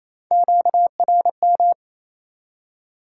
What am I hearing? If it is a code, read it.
Morse code, heard as a keyed tone, about 28 wpm, 700 Hz. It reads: QRM
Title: Top 90 QSO Elements 28wpm